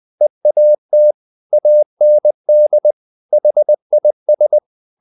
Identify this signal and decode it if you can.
Morse code, heard as a keyed tone, about 20 wpm, 600 Hz. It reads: EAT AND HIS
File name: the old man and the sea part 2 20wpm